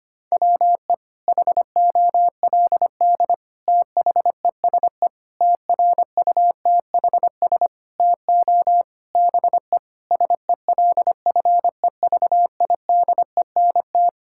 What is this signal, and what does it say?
Morse code, heard as a keyed tone, about 25 wpm, 700 Hz. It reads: WE HOLD THESE TRUTHS TO BE SELFEVIDENT
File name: Sets of 9 Words - 25wpm